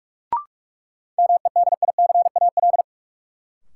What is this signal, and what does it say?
Morse code, heard as a keyed tone, about 43 wpm, 700 Hz. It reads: MEDICAL
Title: Sentences from Top 800 Words - New Word Drill - 45wpm